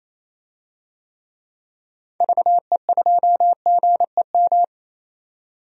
Morse code, heard as a keyed tone, about 28 wpm, 700 Hz. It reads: VE2GEM
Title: DXCC-100 Call Signs - 6 Characters - 28wpm